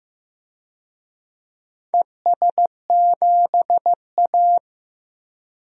Morse code, heard as a keyed tone, about 15 wpm, 700 Hz. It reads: ES7A
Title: DXCC-100 Call Signs - 4 Characters - 15wpm